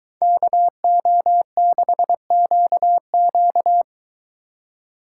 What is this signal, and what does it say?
Morse code, heard as a keyed tone, about 23 wpm, 700 Hz. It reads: KO6QQ